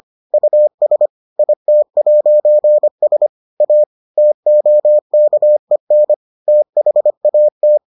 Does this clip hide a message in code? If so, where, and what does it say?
Yes: Morse code from 0.3 to 7.8 s, heard as a keyed tone, about 25 wpm, 600 Hz. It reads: US IT'S A TOKEN THAT